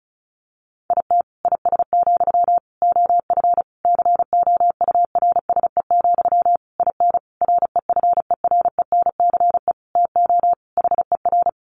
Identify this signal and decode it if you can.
Morse code, heard as a keyed tone, about 34 wpm, 700 Hz. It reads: IT IS, OF COURSE, IN REFERENCE TO HER
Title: Sentences from Top 1200 Words 35wpm